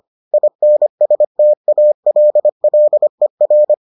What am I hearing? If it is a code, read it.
Morse code, heard as a keyed tone, about 25 wpm, 600 Hz. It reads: INSTALLER